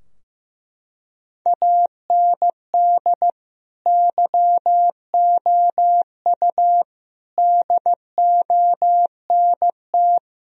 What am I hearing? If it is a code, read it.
Morse code, heard as a keyed tone, about 15 wpm, 700 Hz. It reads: AND YOU DONT
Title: Sentences from Top 1000 Words - Review Part 2 - 15wpm